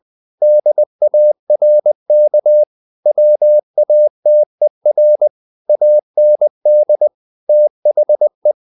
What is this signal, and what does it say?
Morse code, heard as a keyed tone, about 20 wpm, 600 Hz. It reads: DARK WATER AND THE
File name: the old man and the sea / the old man and the sea part 1 20wpm